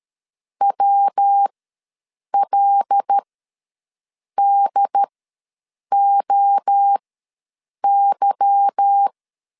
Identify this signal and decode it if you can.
Morse code, heard as a keyed tone, about 13 wpm, spaced as for 7 wpm, 800 Hz. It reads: WLDOY
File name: LEZIONI / 8 YJ45 / 8 Y ws 12wpm 3409